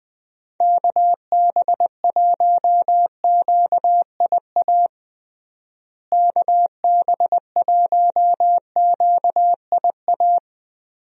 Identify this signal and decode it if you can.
Morse code, heard as a keyed tone, about 20 wpm, 700 Hz. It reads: KB1QIA KB1QIA
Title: USA Call Signs - 6 Characters - Repeated 3x - 20wpm